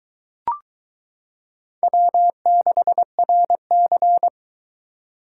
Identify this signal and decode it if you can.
Morse code, heard as a keyed tone, about 22 wpm, 700 Hz. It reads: W6RC